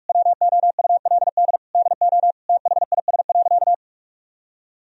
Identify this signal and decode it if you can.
Morse code, heard as a keyed tone, about 46 wpm, 700 Hz. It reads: WOULD DO THIS.
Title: Sentences from Top 1000 Words - Review Part 3 - 45wpm